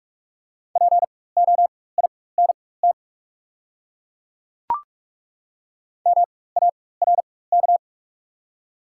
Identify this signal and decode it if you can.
Morse code, heard as a keyed tone, about 43 wpm, spaced as for 21 wpm, 700 Hz. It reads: POINT MARK
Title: 4x Character Spacing - Top 500 Words - 45wpm